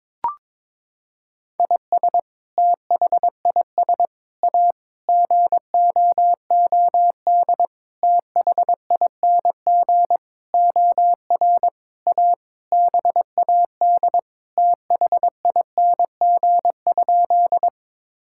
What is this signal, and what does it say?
Morse code, heard as a keyed tone, about 22 wpm, 700 Hz. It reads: IS THIS A GOOD THING OR A BAD THING?
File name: Sentences with 7-Word Phrases - 22wpm